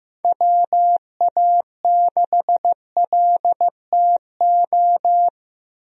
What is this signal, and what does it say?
Morse code, heard as a keyed tone, about 15 wpm, 700 Hz. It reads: WA6LTO
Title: USA Call Signs - 6 Characters - Repeated 2x - 15wpm